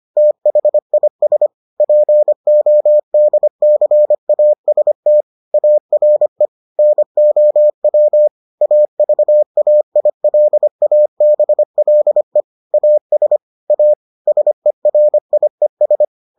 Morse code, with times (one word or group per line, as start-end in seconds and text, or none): 0.2-1.5 THIS
1.8-5.2 PODCAST
5.5-6.5 ARE
6.8-8.3 NOW
8.6-12.4 AVAILABLE
12.7-13.4 AS
13.7-13.9 A
14.3-16.1 SERIES